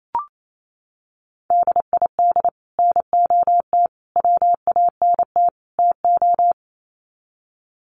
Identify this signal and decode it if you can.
Morse code, heard as a keyed tone, about 28 wpm, 700 Hz. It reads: DID NOT WANT TO
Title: Sets of 4 Words 28wpm